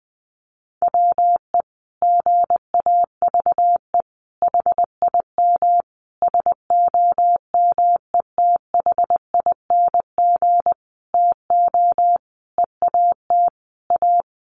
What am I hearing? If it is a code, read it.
Morse code, heard as a keyed tone, about 20 wpm, 700 Hz. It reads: WE GAVE HIM SOMETHING TO EAT A
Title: Sentences from Top 600 Words 20wpm